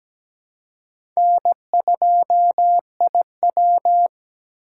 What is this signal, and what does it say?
Morse code, heard as a keyed tone, about 17 wpm, 700 Hz. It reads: N2IW
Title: USA Call Signs - 4 Characters - 3x Character Spacing - 17wpm